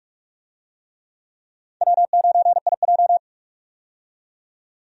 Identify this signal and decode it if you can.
Morse code, heard as a keyed tone, about 46 wpm, 700 Hz. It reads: W9IJ
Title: USA Call Signs - 4 Characters - 45wpm